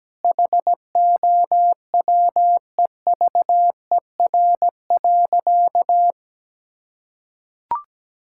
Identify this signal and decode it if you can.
Morse code, heard as a keyed tone, about 17 wpm, 700 Hz. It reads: HOWEVER.
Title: Sentences from Top 1000 Words - Review Part 6 - 17wpm